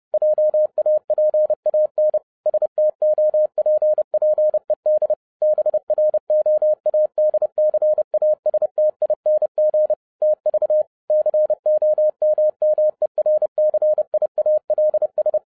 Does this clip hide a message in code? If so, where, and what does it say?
Yes: Morse code from 0.1 to 15.4 s, heard as a keyed tone, about 30 wpm, 610 Hz. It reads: JAPAN STOPPED BROADCASTING TV COMMERCIALS